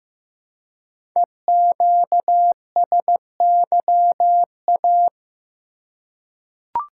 Morse code, heard as a keyed tone, about 15 wpm, 700 Hz. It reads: EQSYA
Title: Code Groups - 5 Letters - 15wpm